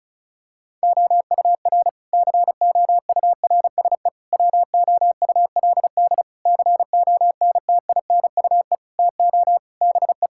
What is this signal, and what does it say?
Morse code, heard as a keyed tone, about 36 wpm, 700 Hz. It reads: OUR COURSE WOULD CONTINUE TO BE
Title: Sentences from Top 1000 Words - Review Part 6 - 35wpm